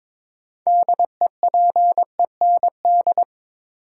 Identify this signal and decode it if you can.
Morse code, heard as a keyed tone, about 22 wpm, 700 Hz. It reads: DEPEND
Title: Sentences from Top 1300 Words - New Word Drill - 22wpm